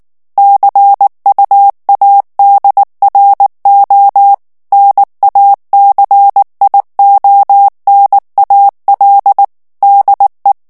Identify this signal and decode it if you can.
Morse code, heard as a keyed tone, about 19 wpm, 800 Hz. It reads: CUADRO NACIONAL DE
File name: morse mp3 / ESPAGNOL TEXTE EN CLAIR 2 VITESSE 18 MOTS MINUTES